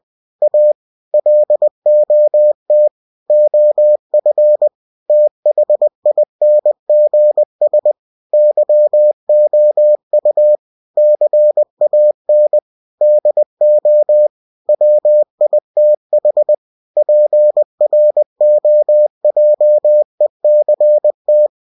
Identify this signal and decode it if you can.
Morse code, heard as a keyed tone, about 20 wpm, 600 Hz. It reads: A LOT OF THINGS YOU CAN DO WITH PROJECT
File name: cw-books / winnie-the-pooh / winnie-20wpm-page-0113